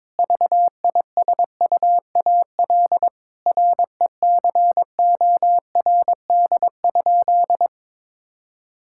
Morse code, heard as a keyed tone, about 22 wpm, 700 Hz. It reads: VISUAL RECORD?
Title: Sentences from Top 1000 Words - Review Part 6 - 22wpm